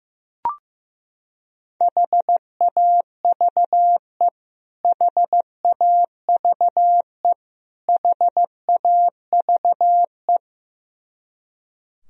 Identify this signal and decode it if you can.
Morse code, heard as a keyed tone, about 15 wpm, 700 Hz. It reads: HAVE HAVE HAVE
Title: Top 100 Words Repeated 15wpm